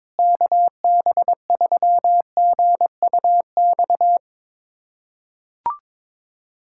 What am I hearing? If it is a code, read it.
Morse code, heard as a keyed tone, about 22 wpm, 700 Hz. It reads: KB3GUX
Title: USA Callsign DE Callsign - 6 Characters - 22wpm